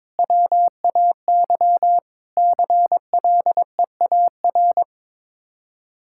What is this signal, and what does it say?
Morse code, heard as a keyed tone, about 22 wpm, 700 Hz. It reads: WAY CLEAR
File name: practice / 22wpm / Sets of 8 Words - 22wpm